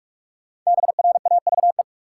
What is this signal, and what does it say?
Morse code, heard as a keyed tone, about 44 wpm, 700 Hz. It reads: BRAVE
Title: Five Letter Words - 45wpm